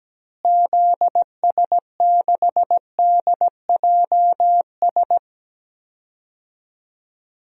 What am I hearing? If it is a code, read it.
Morse code, heard as a keyed tone, about 17 wpm, 700 Hz. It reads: ZS6DJS
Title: DXCC-100 Call Signs - 6 Characters - Repeated 2x - 17wpm